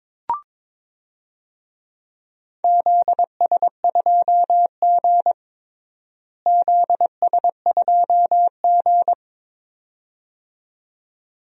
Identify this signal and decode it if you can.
Morse code, heard as a keyed tone, about 22 wpm, 700 Hz. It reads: ZS2G ZS2G
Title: DXCC-100 Call Signs - 4 Characters - Repeated 2x - 22wpm